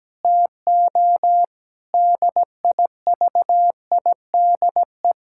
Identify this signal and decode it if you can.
Morse code, heard as a keyed tone, about 17 wpm, 700 Hz. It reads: TO DIVIDE
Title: Sentences from Top 1600 Words 17wpm